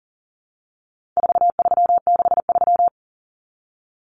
Morse code, heard as a keyed tone, about 40 wpm, 700 Hz. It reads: 4363
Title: Common Cut Numbers - 4-Digit 40wpm